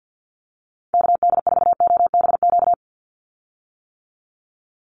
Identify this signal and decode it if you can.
Morse code, heard as a keyed tone, about 50 wpm, 700 Hz. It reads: KD4OBQ
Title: USA Call Signs 50wpm